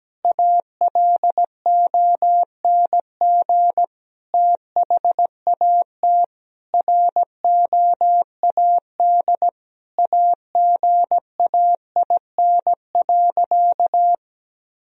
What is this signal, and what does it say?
Morse code, heard as a keyed tone, about 17 wpm, 700 Hz. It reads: ALONG THAT ROAD AGAIN.
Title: Sentences from Top 1000 Words - Review Part 2 - 17wpm